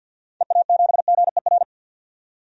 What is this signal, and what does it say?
Morse code, heard as a keyed tone, about 51 wpm, 700 Hz. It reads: EA7GER
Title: DXCC-100 Call Signs - 6 Characters - 50wpm